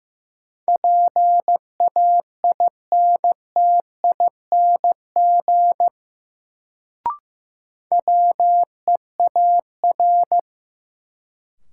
Morse code, heard as a keyed tone, about 15 wpm, 700 Hz. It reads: PAINTING WEAR
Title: Sentences from Top 1000 Words - Review Part 6 - New Word Drill - 15wpm